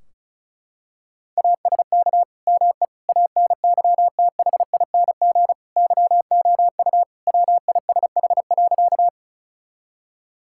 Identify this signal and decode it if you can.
Morse code, heard as a keyed tone, about 36 wpm, 700 Hz. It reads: ASK ME ANYTHING YOU WISH.
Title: Sentences from Top 1600 Words 35wpm